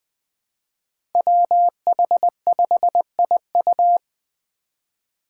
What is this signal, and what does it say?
Morse code, heard as a keyed tone, about 20 wpm, 700 Hz. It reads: WH5IU
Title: Five Challenging Letter-Number Combinations 20wpm